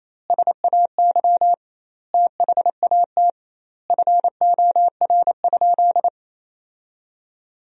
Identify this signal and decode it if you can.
Morse code, heard as a keyed tone, about 28 wpm, 700 Hz. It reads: SAY THAT FOR?
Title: 2x Word-Spacing - Sentences from Top 200 Words - 28wpm